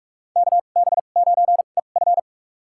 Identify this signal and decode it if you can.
Morse code, heard as a keyed tone, about 47 wpm, spaced as for 33 wpm, 700 Hz. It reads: KB9EF
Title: USA Call Signs - 2x Character Spacing - 45wpm